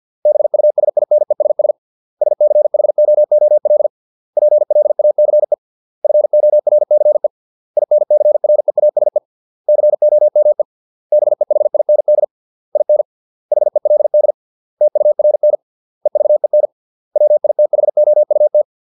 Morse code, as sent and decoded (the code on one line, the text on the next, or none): -... ..- ... .. -. ... ... / ... -.-. .... --- --- .-.. / .--. .-.. .- -.-. / ..-. --- .-. -.-. / .. -. -.-. .-. .- ... / -.-. --- -- . / -... .... .. -. -.. / .. -. / .... .-.. -.. / - ..- .-. -. / . ...- . -. / .-- .. - .... --- ..- -
BUSINESS SCHOOL PLACE FORCE INCREASE COME BEHIND IN HELD TURN EVEN WITHOUT